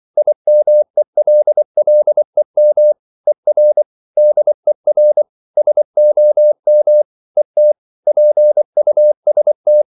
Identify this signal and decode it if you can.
Morse code, heard as a keyed tone, about 24 wpm, 600 Hz. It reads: IMELLEM ER DER SOM ET PUST
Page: Ferie 24 wpm